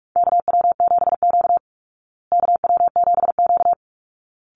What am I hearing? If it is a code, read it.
Morse code, heard as a keyed tone, about 44 wpm, 700 Hz. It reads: KW7Q KW7Q
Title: Call Signs Repeated 3x - 45wpm